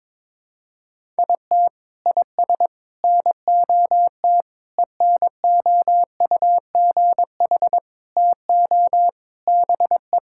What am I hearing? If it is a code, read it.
Morse code, heard as a keyed tone, about 22 wpm, 700 Hz. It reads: IT IS NOT ENOUGH TO BE
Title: Sentences from Top 1000 Words - Review Part 6 - 22wpm